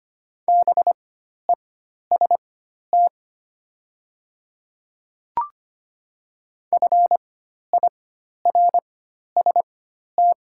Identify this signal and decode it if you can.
Morse code, heard as a keyed tone, about 25 wpm, spaced as for 12 wpm, 700 Hz. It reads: BEST FIRST
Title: 4x Character Spacing - Top 500 Words - 25wpm